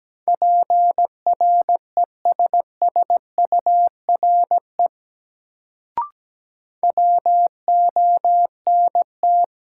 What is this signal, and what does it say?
Morse code, heard as a keyed tone, about 17 wpm, 700 Hz. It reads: PRESSURE WONT
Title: Sentences from Top 1000 Words - Review Part 6 - New Word Drill - 17wpm